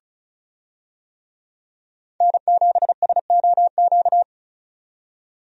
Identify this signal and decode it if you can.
Morse code, heard as a keyed tone, about 36 wpm, 700 Hz. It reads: N7SOQ
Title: USA Call Signs - 5 Characters - 35wpm